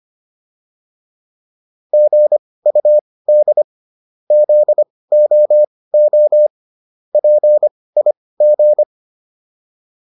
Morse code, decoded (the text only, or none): GUD ZOO PIG